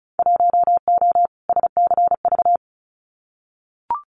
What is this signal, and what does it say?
Morse code, heard as a keyed tone, about 34 wpm, 700 Hz. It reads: 1O SCV
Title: Winter Field Day Exchanges 35wpm